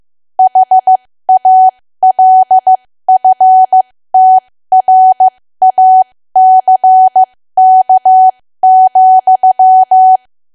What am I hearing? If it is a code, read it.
Morse code, heard as a keyed tone, about 15 wpm, spaced as for 13 wpm, 750 Hz. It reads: HALFTRACK,
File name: cwa / W1AW-Level2 / 13WPM